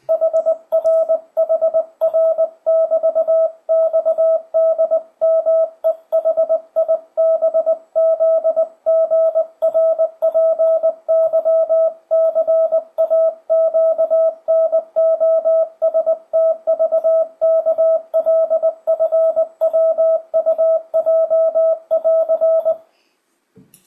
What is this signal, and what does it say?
Morse code, heard as a keyed tone, about 20 wpm, 650 Hz, with radio noise. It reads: HR HR<BT>XDMEHIBZGRPYCAQNOSTVKLFWUJ<AR>
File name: 2025cw / speedup / 20wpm